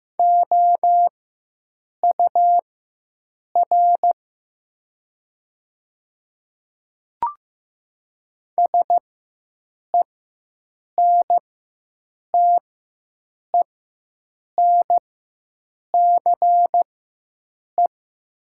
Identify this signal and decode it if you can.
Morse code, heard as a keyed tone, about 15 wpm, spaced as for 7 wpm, 700 Hz. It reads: OUR SENTENCE
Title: 4x Character Spacing - Top 200 Words - 15wpm